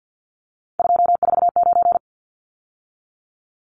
Morse code, heard as a keyed tone, about 50 wpm, 700 Hz. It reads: 249
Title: Numbers - 3-Digit 50wpm